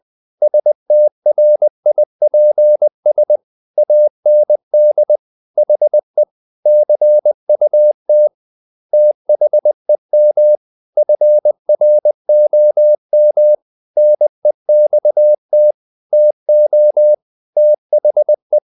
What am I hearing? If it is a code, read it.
Morse code, heard as a keyed tone, about 20 wpm, 600 Hz. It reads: STRIPS AND HE CUT THEM FROM NEXT TO THE
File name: the old man and the sea / the old man and the sea part 1 20wpm